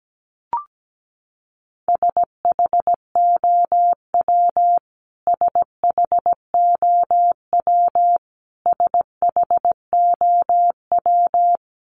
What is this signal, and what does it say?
Morse code, heard as a keyed tone, about 17 wpm, 700 Hz. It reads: SHOW SHOW SHOW
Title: Top 300 Words Repeated 17wpm